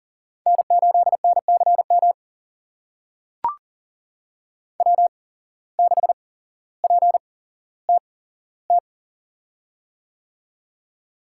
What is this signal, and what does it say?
Morse code, heard as a keyed tone, about 40 wpm, spaced as for 17 wpm, 700 Hz. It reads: N8NCM W6PTT
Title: USA Call Signs - 5 Characters - 8x Character Spacing - 40wpm